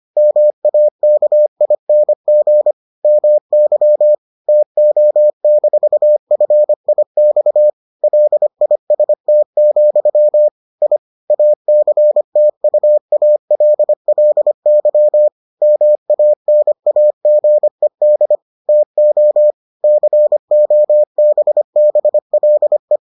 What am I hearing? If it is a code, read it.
Morse code, heard as a keyed tone, about 25 wpm, 600 Hz. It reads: MAKING MY TO-FIX LIST, I ACTUALLY MANAGED TO COBBLE